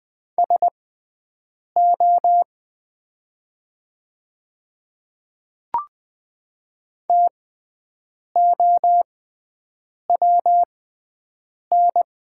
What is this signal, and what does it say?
Morse code, heard as a keyed tone, about 21 wpm, spaced as for 7 wpm, 700 Hz. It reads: SO TOWN
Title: 6x Character Spacing - Top 500 Words - 20wpm